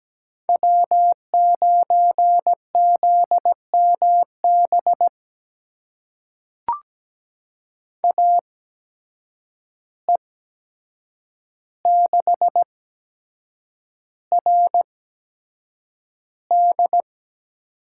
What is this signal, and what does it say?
Morse code, heard as a keyed tone, about 17 wpm, spaced as for 7 wpm, 700 Hz. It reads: W9ZMB AE6RD
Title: USA Call Signs - 5 Characters - 8x Character Spacing - 17wpm